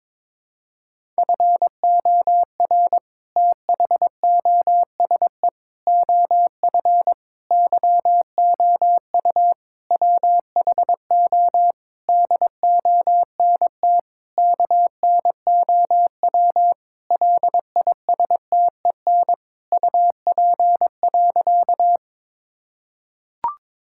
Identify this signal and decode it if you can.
Morse code, heard as a keyed tone, about 22 wpm, 700 Hz. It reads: FOR THOSE OF YOU WHO DONT KNOW LISTEN UP.
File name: Sentences with 7-Word Phrases - 22wpm